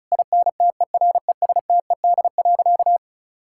Morse code, heard as a keyed tone, about 36 wpm, 700 Hz. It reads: INTERESTED.
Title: Sentences from Top 1000 Words - Review Part 6 - 35wpm